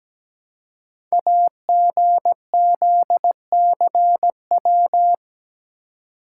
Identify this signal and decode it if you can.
Morse code, heard as a keyed tone, about 17 wpm, 700 Hz. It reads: AGZCW
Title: Code Groups - 5 Letters - 17wpm